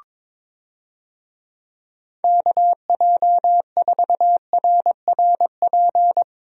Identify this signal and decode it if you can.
Morse code, heard as a keyed tone, about 22 wpm, 700 Hz. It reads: KJ4RRP
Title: USA Call Signs - 6 Characters - Repeated 2x - 22wpm